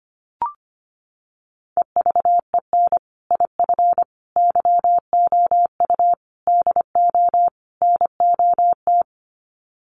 Morse code, heard as a keyed tone, about 25 wpm, 700 Hz. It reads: EVEN IF YOU DO NOT
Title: Sets of 5 Words Encore 25wpm